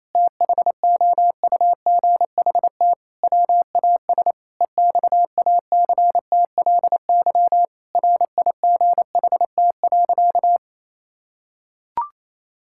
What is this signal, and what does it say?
Morse code, heard as a keyed tone, about 28 wpm, 700 Hz. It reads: THOUGHT WAS EXACTLY RIGHT.
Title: Sentences from Top 1000 Words - Review Part 6 - 28wpm